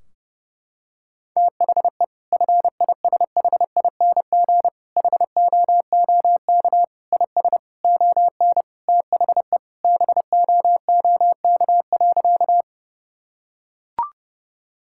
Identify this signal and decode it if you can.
Morse code, heard as a keyed tone, about 32 wpm, 700 Hz. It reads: THE FISHING HOOK IS ON THE BOOK.
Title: Sentences from Top 1600 Words 30wpm